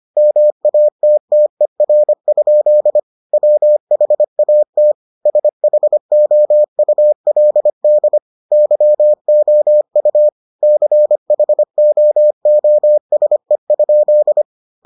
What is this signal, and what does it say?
Morse code, heard as a keyed tone, about 25 wpm, 600 Hz. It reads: MATTER? WHAT SHOULD YOU CHOOSE?